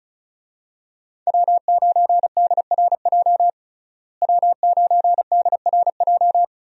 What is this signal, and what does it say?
Morse code, heard as a keyed tone, about 34 wpm, 700 Hz. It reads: W9DRJ W9DRJ
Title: USA Call Signs - 5 Characters - Repeated 3x - 35wpm